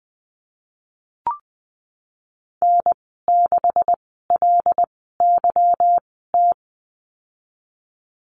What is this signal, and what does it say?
Morse code, heard as a keyed tone, about 19 wpm, spaced as for 14 wpm, 700 Hz. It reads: N6LYT
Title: 167 Lesson 27g - Number 7 - Call Signs 20wpm